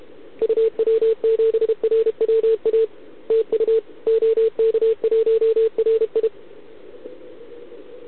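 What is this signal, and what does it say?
Morse code, heard as a keyed tone, about 32 wpm, 440 Hz. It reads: UW7RWA TU OK1RI
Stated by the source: Contest CW from OK1RI on 80 m during the OKOM DX contest, recorded by a QRP station using a KX3 and a wire antenna.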